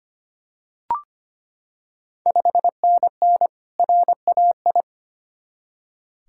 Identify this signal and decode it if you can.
Morse code, heard as a keyed tone, about 25 wpm, 700 Hz. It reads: 5NN RAI
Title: Ontario QSO Party Exchange 25wpm